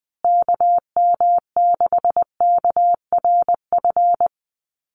Morse code, heard as a keyed tone, about 20 wpm, 700 Hz. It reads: KM6KRF
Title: USA Call Signs Repeated 3x 20wpm